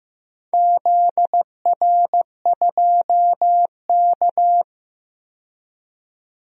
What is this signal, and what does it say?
Morse code, heard as a keyed tone, about 15 wpm, 700 Hz. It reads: ZR2K
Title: DXCC-100 Call Signs - 4 Characters - 15wpm